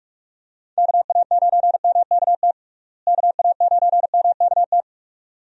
Morse code, heard as a keyed tone, about 44 wpm, 700 Hz. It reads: KA9MKT KA9MKT
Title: USA Call Signs - 6 Characters - Repeated 2x - 45wpm